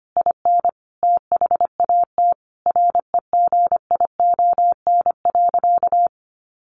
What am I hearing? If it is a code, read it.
Morse code, heard as a keyed tone, about 25 wpm, 700 Hz. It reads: IN THAT REGION.
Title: A Sound Of Thunder 25wpm